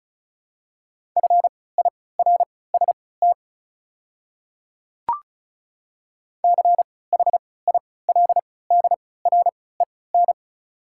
Morse code, heard as a keyed tone, about 36 wpm, spaced as for 20 wpm, 700 Hz. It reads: FIRST CHILDREN
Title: 3x Character Spacing - Top 300 Words - 35wpm